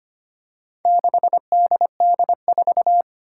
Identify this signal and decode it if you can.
Morse code, heard as a keyed tone, about 25 wpm, 700 Hz. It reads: 6DD4
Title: Four Challenging Letter-Number Combinations 25wpm